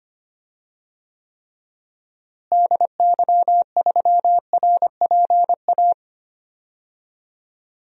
Morse code, heard as a keyed tone, about 25 wpm, 700 Hz. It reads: DY3RPA
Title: DXCC-100 Call Signs - 6 Characters - Repeated 2x - 25wpm